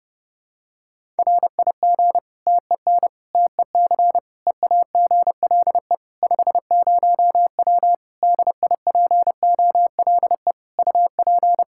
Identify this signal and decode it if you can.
Morse code, heard as a keyed tone, about 30 wpm, 700 Hz. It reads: RIG TEN TEC EAGLE 50W DIPOLE UP
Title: Practice QSO - Exchange 2 - Info - 30wpm